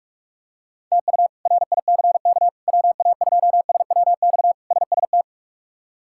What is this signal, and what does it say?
Morse code, heard as a keyed tone, about 46 wpm, 700 Hz. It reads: TU RICK WA2SWX SST
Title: K1USN SST QSO Practice - 45wpm